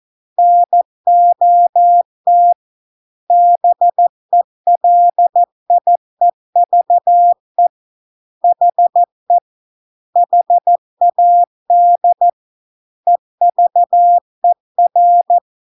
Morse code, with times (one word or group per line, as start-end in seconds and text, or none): 0.4-2.5 NOT
3.3-7.7 BELIEVE
8.4-9.4 HE
10.2-12.3 HAD
13.1-15.4 EVER